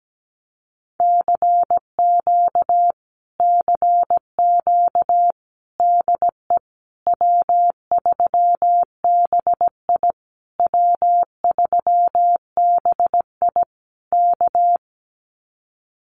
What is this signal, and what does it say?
Morse code, heard as a keyed tone, about 17 wpm, 700 Hz. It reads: CQ CQ DE W3BI W3BI K
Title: Common QSO Phrases Encore 17wpm